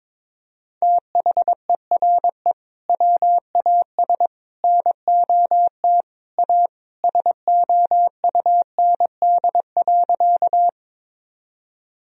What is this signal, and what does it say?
Morse code, heard as a keyed tone, about 22 wpm, 700 Hz. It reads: THERE WAS NOT A SOUND.
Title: Sentences from Top 1000 Words - Review Part 1 - 22wpm